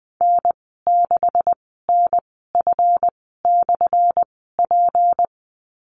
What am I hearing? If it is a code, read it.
Morse code, heard as a keyed tone, about 20 wpm, spaced as for 15 wpm, 700 Hz. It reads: N6NF/P
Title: 179 Lesson 29g - Slash - Call Signs 20wpm